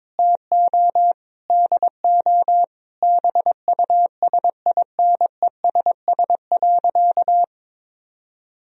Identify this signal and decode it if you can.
Morse code, heard as a keyed tone, about 22 wpm, 700 Hz. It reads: TO DO BUSINESS.
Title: Sentences from Top 1000 Words - Review Part 3 - 22wpm